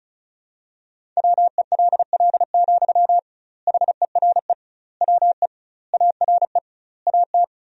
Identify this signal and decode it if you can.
Morse code, heard as a keyed tone, about 36 wpm, 700 Hz. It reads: WELL, HERE WE ARE AT
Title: 2x Word-Spacing - Sentences from Top 300 Words - 35wpm